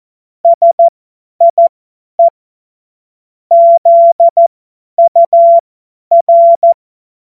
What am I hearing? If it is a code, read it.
Morse code, heard as a keyed tone, about 14 wpm, spaced as for 10 wpm, 680 Hz. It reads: SIE ZUR
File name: Witz-21-Oktober-2025-50-F70